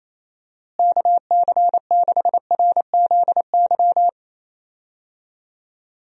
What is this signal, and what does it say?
Morse code, heard as a keyed tone, about 27 wpm, 700 Hz. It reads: KC6RZY
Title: USA Call Signs - 6 Characters - Repeated 3x - 28wpm